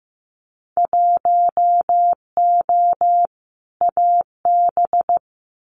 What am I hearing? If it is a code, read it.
Morse code, heard as a keyed tone, about 15 wpm, 700 Hz. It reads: 1O AB
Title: Winter Field Day Exchanges 15wpm